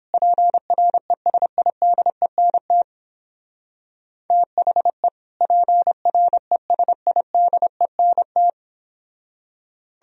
Morse code, heard as a keyed tone, about 29 wpm, 700 Hz. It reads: PRESIDENT THE PRESIDENT